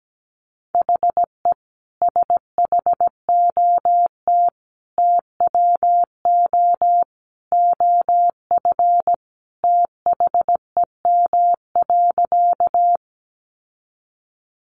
Morse code, heard as a keyed tone, about 17 wpm, 700 Hz. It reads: HE SHOT TWO OF THEM.
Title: Sentences from Top 1100 Words 17wpm